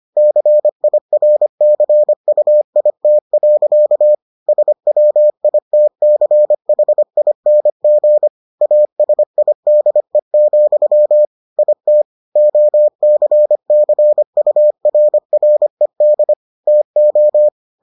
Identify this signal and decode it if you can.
Morse code, heard as a keyed tone, about 25 wpm, 600 Hz. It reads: CIRCUIT. SWITCHING ASIDE, IT OCCURRED TO